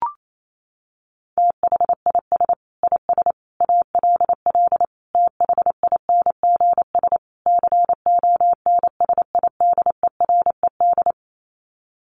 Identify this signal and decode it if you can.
Morse code, heard as a keyed tone, about 28 wpm, 700 Hz. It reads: THIS IS ALL THINGS CONSIDERED
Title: Sets of 5 Words 28wpm